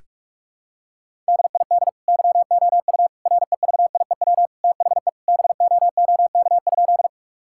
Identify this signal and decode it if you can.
Morse code, heard as a keyed tone, about 46 wpm, 700 Hz. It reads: DID YOU REVIEW THE BOOK?
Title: Sentences from Top 1000 Words - Review Part 6 - 45wpm